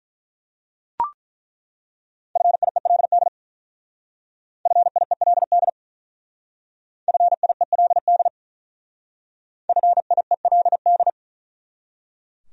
Morse code, heard as a keyed tone, about 42 wpm, 700 Hz. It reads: FIELD FIELD FIELD FIELD
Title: Speed-Racing Top 500 Words 35wpm